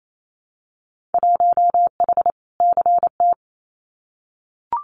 Morse code, heard as a keyed tone, about 28 wpm, 700 Hz. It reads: 1H CT
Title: Winter Field Day Exchanges 28wpm